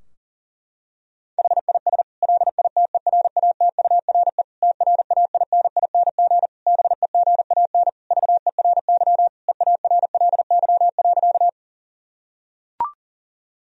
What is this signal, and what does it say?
Morse code, heard as a keyed tone, about 40 wpm, 700 Hz. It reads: HIS LITERATURE TRAINING BEGAN VERY EARLY.
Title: Sentences from Top 1000 Words - Review Part 4 - 40wpm